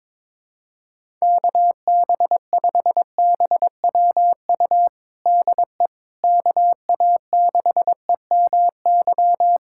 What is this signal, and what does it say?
Morse code, heard as a keyed tone, about 22 wpm, 700 Hz. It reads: KB5BWU DE KA6EMY
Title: USA Callsign DE Callsign - 6 Characters - 22wpm